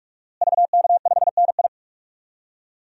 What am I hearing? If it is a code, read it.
Morse code, heard as a keyed tone, about 47 wpm, 700 Hz. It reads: VK5NI
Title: DXCC-100 Call Signs - 5 Characters - Repeated 2x - 45wpm